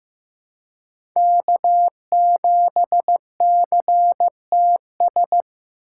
Morse code, heard as a keyed tone, about 15 wpm, 700 Hz. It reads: K7CTS
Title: USA Call Signs - 5 Characters - 2x Character Spacing - 15wpm